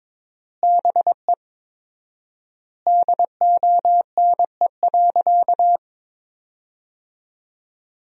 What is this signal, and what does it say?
Morse code, heard as a keyed tone, about 22 wpm, 700 Hz. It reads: BE DONE.
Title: 4x Word-Spacing - Sentences from Top 500 Words - 22wpm